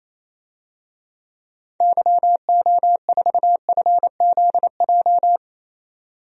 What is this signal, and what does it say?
Morse code, heard as a keyed tone, about 28 wpm, 700 Hz. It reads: YO4FZJ
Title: DXCC-100 Call Signs - 6 Characters - 28wpm